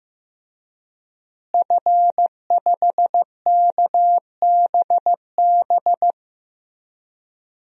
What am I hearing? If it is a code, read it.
Morse code, heard as a keyed tone, about 15 wpm, 700 Hz. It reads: F5KBB
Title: DXCC-100 Call Signs - 5 Characters - Repeated 3x - 15wpm